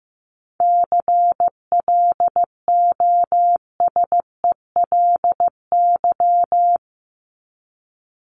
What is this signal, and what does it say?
Morse code, heard as a keyed tone, about 15 wpm, 700 Hz. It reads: CLOSELY
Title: Top 1251-1500 Words 15wpm